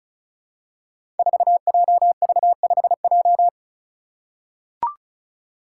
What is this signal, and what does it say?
Morse code, heard as a keyed tone, about 36 wpm, 700 Hz. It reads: 4JV5J
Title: Five Challenging Letter-Number Combinations 35wpm